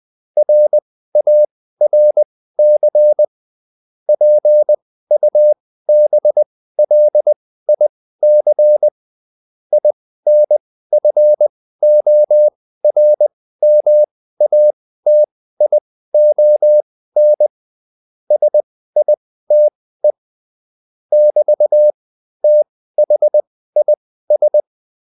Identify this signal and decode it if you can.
Morse code, heard as a keyed tone, about 20 wpm, spaced as for 14 wpm, 600 Hz. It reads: RARC PUBLIC INFORMATION SITE <BT>THIS